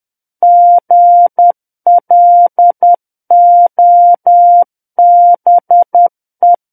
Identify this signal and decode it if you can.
Morse code, heard as a keyed tone, about 10 wpm, 700 Hz. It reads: GLOBE
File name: Book One / 01-Chapter 1 - 10 WPM